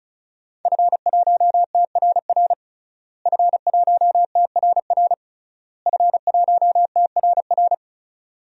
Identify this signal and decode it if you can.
Morse code, heard as a keyed tone, about 36 wpm, 700 Hz. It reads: F1TRR F1TRR F1TRR
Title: DXCC-100 Call Signs - 5 Characters - Repeated 3x - 35wpm